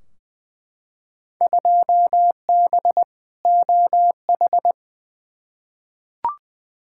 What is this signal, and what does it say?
Morse code, heard as a keyed tone, about 20 wpm, 700 Hz. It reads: 2B OH
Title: Field Day Exchanges Encore - 20wpm